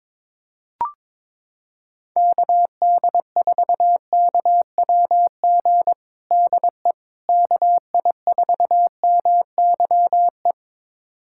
Morse code, heard as a keyed tone, about 22 wpm, 700 Hz. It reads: KD4KWG DE KI4MYE
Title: USA Callsign DE Callsign 22wpm